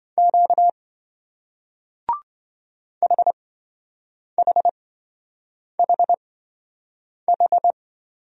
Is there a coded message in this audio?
Yes: Morse code from 0.2 to 7.7 s, heard as a keyed tone, about 27 wpm, spaced as for 7 wpm, 700 Hz. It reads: Q HHHH